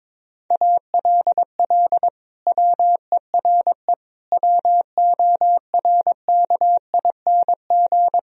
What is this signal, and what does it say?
Morse code, heard as a keyed tone, about 22 wpm, 700 Hz. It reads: ALL WERE WORKING
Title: Sentences from Top 1000 Words - Review Part 4 - 22wpm